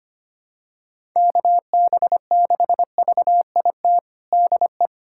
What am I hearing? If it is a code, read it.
Morse code, heard as a keyed tone, about 25 wpm, 700 Hz. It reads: KB6VIT DE
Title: USA Callsign DE Callsign - 6 Characters - 25wpm